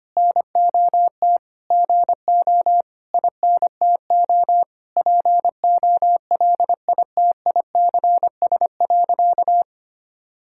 Morse code, heard as a keyed tone, about 25 wpm, 700 Hz. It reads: NOT GO INTO POLITICS.
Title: Sentences from Top 1000 Words - Review Part 6 - 25wpm